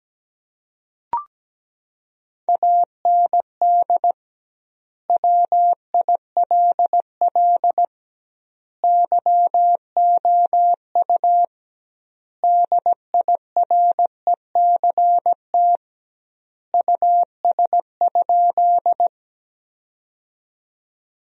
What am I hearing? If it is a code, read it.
Morse code, heard as a keyed tone, about 17 wpm, 700 Hz. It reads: AND WILL YOU DIRECT US?
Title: 2x Word-Spacing - Sentences from Top 400 Words - 17wpm